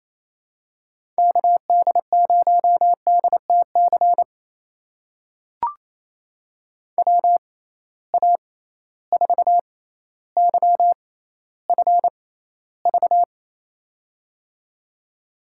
Morse code, heard as a keyed tone, about 28 wpm, spaced as for 14 wpm, 700 Hz. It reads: KD0DTC WA4YFV
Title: USA Call Signs - 6 Characters - 6x Character Spacing - 28wpm